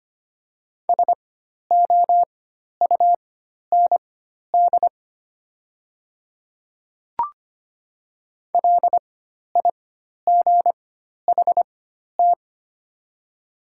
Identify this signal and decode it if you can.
Morse code, heard as a keyed tone, about 25 wpm, spaced as for 12 wpm, 700 Hz. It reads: SOUND LIGHT
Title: 4x Character Spacing - Top 200 Words - 25wpm